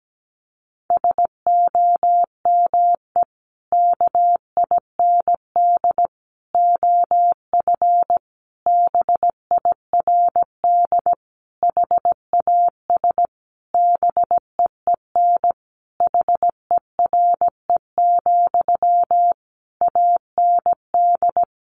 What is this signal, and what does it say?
Morse code, heard as a keyed tone, about 17 wpm, 700 Hz. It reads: SOME KIND OF BIRD HAS BEEN HERE, AND
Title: Sentences from Top 1200 Words 17wpm